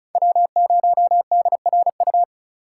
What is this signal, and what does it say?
Morse code, heard as a keyed tone, about 36 wpm, 700 Hz. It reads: W0DRU
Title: USA Call Signs - 5 Characters - 2x Character Spacing - 35wpm